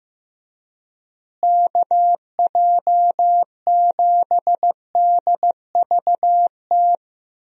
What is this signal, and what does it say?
Morse code, heard as a keyed tone, about 16 wpm, 700 Hz. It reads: KJ7DVT
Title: USA Call Signs - 6 Characters - Repeated 3x - 15wpm